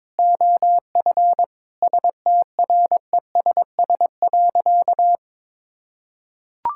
Morse code, heard as a keyed tone, about 22 wpm, 700 Hz. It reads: OF STRESS.
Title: Sentences from Top 1000 Words - Review Part 6 - 22wpm